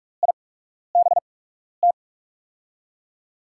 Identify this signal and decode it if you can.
Morse code, heard as a keyed tone, about 47 wpm, spaced as for 12 wpm, 700 Hz. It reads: IBT
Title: USA Call Signs - 6 Characters - 8x Character Spacing - 45wpm